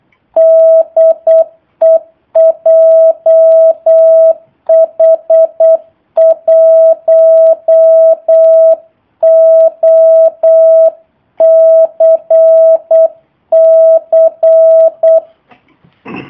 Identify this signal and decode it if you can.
Morse code, heard as a keyed tone, about 8 wpm, 650 Hz. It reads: DE JH1OCC
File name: jh8cbh / 2019cw / sp03- w